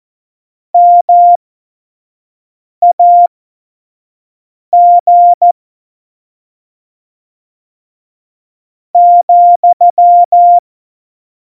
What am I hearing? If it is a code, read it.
Morse code, heard as a keyed tone, about 14 wpm, spaced as for 5 wpm, 700 Hz. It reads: MAG ,
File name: Morsekurs / Witze / Witz 06 March 2018, 25 F70